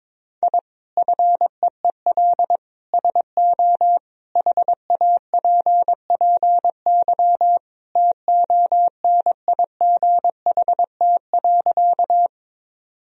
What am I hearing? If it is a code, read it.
Morse code, heard as a keyed tone, about 22 wpm, 700 Hz. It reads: I FEEL SO HAPPY TONIGHT.
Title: Sentences from Top 1000 Words - Review Part 3 - 22wpm